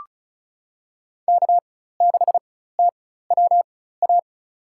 Morse code, heard as a keyed tone, about 36 wpm, spaced as for 16 wpm, 700 Hz. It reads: K6TWA
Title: USA Call Signs - 5 Characters - 4x Character Spacing - 35wpm